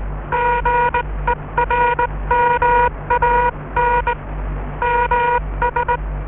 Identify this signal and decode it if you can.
Morse code, heard as a keyed tone, about 19 wpm, 960 Hz, with noise. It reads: GERMAN MS